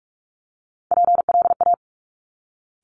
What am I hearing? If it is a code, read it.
Morse code, heard as a keyed tone, about 46 wpm, 700 Hz. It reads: PLA
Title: Top Three Letter Combinations in Words 45wpm